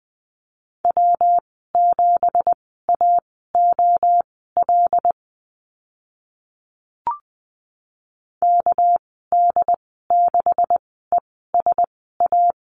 Morse code, heard as a keyed tone, about 19 wpm, spaced as for 14 wpm, 700 Hz. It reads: W7AOL KD6ESA
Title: 213 Lesson 34g - Number 8 - Call Signs 20wpm